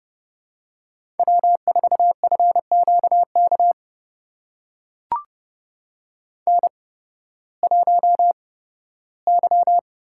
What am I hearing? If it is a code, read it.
Morse code, heard as a keyed tone, about 30 wpm, spaced as for 16 wpm, 700 Hz. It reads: W4FQK N1Y
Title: USA Call Signs - 5 Characters - 8x Character Spacing - 30wpm